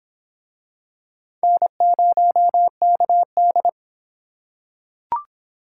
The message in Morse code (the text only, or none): N0KD